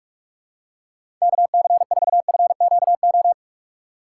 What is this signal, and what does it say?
Morse code, heard as a keyed tone, about 46 wpm, 700 Hz. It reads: KC4FQO
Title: USA Call Signs - 6 Characters - 2x Character Spacing - 45wpm